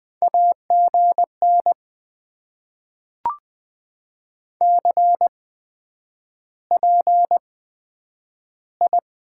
Morse code, heard as a keyed tone, about 20 wpm, spaced as for 9 wpm, 700 Hz. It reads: AGN CPI
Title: Top 90 QSO Elements - 8x Character Spacing - 20wpm